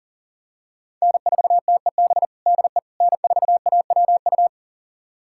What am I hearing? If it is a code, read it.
Morse code, heard as a keyed tone, about 40 wpm, 700 Hz. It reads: N4TEB DE N4AWU
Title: USA Callsign DE Callsign - 5 Characters - 40wpm